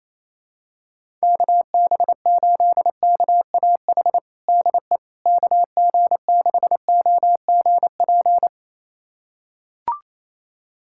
Morse code, heard as a keyed tone, about 28 wpm, 700 Hz. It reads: KB8KAH DE KG6OGP
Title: USA Callsign DE Callsign - 6 Characters - 28wpm